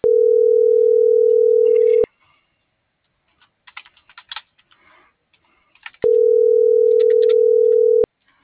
Ringback tone (North America), 440 + 480 Hz.
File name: CENTRAL MAYAB / out raulcristinomayab